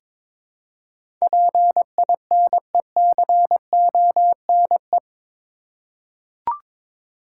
Morse code, heard as a keyed tone, about 21 wpm, 700 Hz. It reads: PINECONE
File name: Difficult Words with Hints - 22wpm